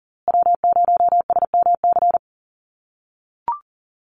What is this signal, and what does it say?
Morse code, heard as a keyed tone, about 40 wpm, 700 Hz. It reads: W0SMC
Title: USA Call Signs Repeated 2x 40wpm